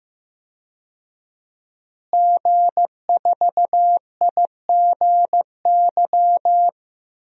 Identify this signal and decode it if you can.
Morse code, heard as a keyed tone, about 15 wpm, 700 Hz. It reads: G4IGY
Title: DXCC-100 Call Signs - 5 Characters - Repeated 2x - 15wpm